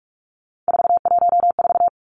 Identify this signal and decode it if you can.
Morse code, heard as a keyed tone, about 46 wpm, 700 Hz. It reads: 414
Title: Three Challenging Letter-Number Combinations 45wpm